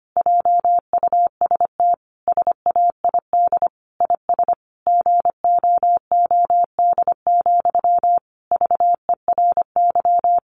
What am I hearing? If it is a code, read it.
Morse code, heard as a keyed tone, about 25 wpm, 700 Hz. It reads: JUST SAID IS GOOD, VERY
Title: Sentences from Top 200 Words 25wpm